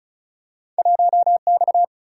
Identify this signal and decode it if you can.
Morse code, heard as a keyed tone, about 36 wpm, 700 Hz. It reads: 1X
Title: Single-Letter Single-Number 35wpm